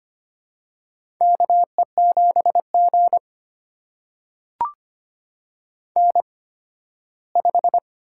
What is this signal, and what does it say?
Morse code, heard as a keyed tone, about 25 wpm, spaced as for 15 wpm, 700 Hz. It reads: KE7G N5
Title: USA Call Signs - 8x Character Spacing - 25wpm